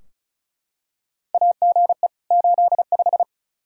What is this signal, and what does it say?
Morse code, heard as a keyed tone, about 36 wpm, 700 Hz. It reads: AGE 85
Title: Practice QSO - Exchange 3 - More Info - 35wpm